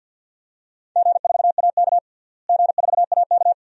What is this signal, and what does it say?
Morse code, heard as a keyed tone, about 52 wpm, 700 Hz. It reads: G4AK G4AK
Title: DXCC-100 Call Signs - 4 Characters - Repeated 2x - 50wpm